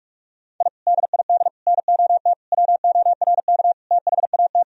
Morse code, heard as a keyed tone, about 44 wpm, 700 Hz. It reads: I DID NOT WORK THAT
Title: Sentences from Top 1000 Words - Review Part 2 - 45wpm